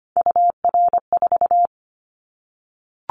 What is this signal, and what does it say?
Morse code, heard as a keyed tone, about 25 wpm, 700 Hz. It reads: UR4
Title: Callsign Prefixes for Top 25 DXCC - Repeated 3x - 25wpm